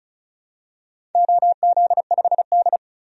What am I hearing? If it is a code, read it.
Morse code, heard as a keyed tone, about 35 wpm, 700 Hz. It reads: OZ5D
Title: DXCC-100 Call Signs - 4 Characters - 35wpm